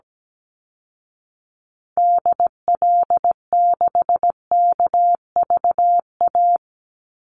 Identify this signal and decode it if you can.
Morse code, heard as a keyed tone, about 17 wpm, 700 Hz. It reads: DL6KVA
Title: Call Signs Repeated 2x - 17wpm